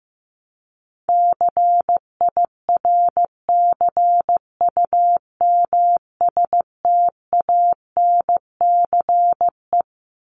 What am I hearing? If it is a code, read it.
Morse code, heard as a keyed tone, about 15 wpm, 700 Hz. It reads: CIRCUMSTANCE
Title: ES:S Suffix - Top 100 Words with Base Words 15wpm